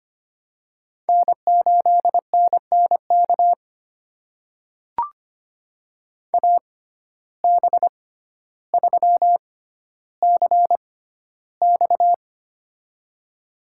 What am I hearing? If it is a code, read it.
Morse code, heard as a keyed tone, about 25 wpm, spaced as for 13 wpm, 700 Hz. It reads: N8NNK AB3CX
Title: USA Call Signs - 5 Characters - 6x Character Spacing - 25wpm